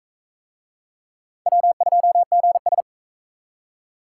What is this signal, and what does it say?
Morse code, heard as a keyed tone, about 40 wpm, 700 Hz. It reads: W2GS